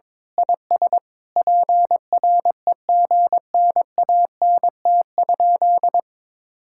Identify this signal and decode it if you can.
Morse code, heard as a keyed tone, about 22 wpm, 700 Hz. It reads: IS PREGNANT?
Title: Sentences from Top 1000 Words - Review Part 6 - 22wpm